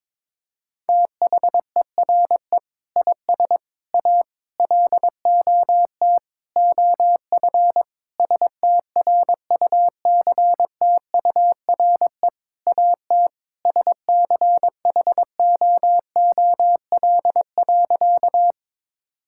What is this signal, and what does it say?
Morse code, heard as a keyed tone, about 22 wpm, 700 Hz. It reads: THERE IS A LOT OF STRUCTURE AT SCHOOL.
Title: Sentences from Top 1000 Words - Review Part 6 - 22wpm